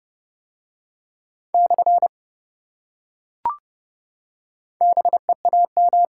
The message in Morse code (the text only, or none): / BEAM